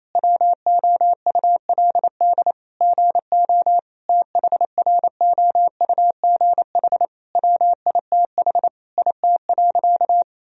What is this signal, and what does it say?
Morse code, heard as a keyed tone, about 27 wpm, 700 Hz. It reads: WOULD GO THROUGH WITH IT.
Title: Sentences from Top 1000 Words - Review Part 3 - 28wpm